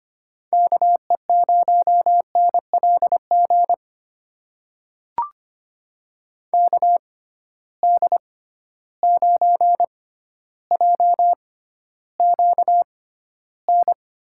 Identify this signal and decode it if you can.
Morse code, heard as a keyed tone, about 25 wpm, spaced as for 13 wpm, 700 Hz. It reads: KE0NLG KD9JQN
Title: USA Call Signs - 6 Characters - 6x Character Spacing - 25wpm